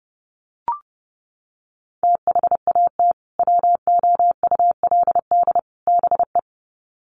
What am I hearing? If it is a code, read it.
Morse code, heard as a keyed tone, about 30 wpm, 700 Hz. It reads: THAT WOULD BE
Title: Sets of 3 Words 30wpm